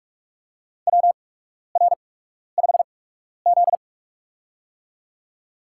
Morse code, heard as a keyed tone, about 48 wpm, spaced as for 12 wpm, 700 Hz. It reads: WR5Z
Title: USA Call Signs - 4 Characters - 8x Character Spacing - 45wpm